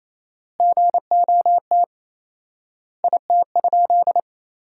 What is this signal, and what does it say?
Morse code, heard as a keyed tone, about 28 wpm, 700 Hz. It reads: GOT IT?
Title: 4x Word-Spacing - Sentences from Top 300 Words - 28wpm